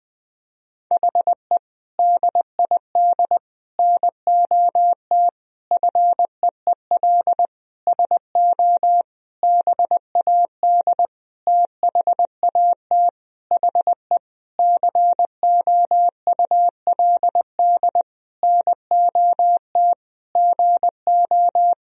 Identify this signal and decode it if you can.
Morse code, heard as a keyed tone, about 20 wpm, 700 Hz. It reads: HE DID NOT FEEL SO BAD THAT HE COULD NOT GO
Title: Sentences from Top 1000 Words - Review Part 4 - 20wpm